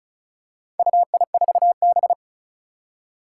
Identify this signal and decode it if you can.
Morse code, heard as a keyed tone, about 36 wpm, 700 Hz. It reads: UI4B
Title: Four Challenging Letter-Number Combinations 35wpm